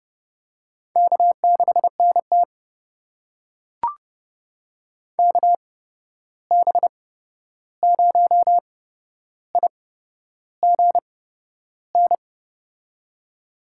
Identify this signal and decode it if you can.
Morse code, heard as a keyed tone, about 30 wpm, spaced as for 11 wpm, 700 Hz. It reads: K6NT KB0IGN
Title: USA Call Signs - 8x Character Spacing - 30wpm